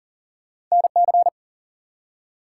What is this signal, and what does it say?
Morse code, heard as a keyed tone, about 43 wpm, 700 Hz. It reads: NC
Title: US States and Canadian Provinces 40wpm